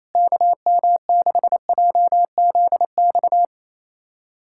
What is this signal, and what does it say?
Morse code, heard as a keyed tone, about 27 wpm, 700 Hz. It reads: KM6JZX
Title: USA Callsign DE Callsign - 6 Characters - 28wpm